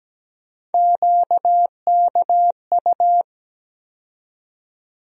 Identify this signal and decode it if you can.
Morse code, heard as a keyed tone, about 17 wpm, 700 Hz. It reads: QKU
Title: USA Call Sign Postfixes - 3 Letters 17wpm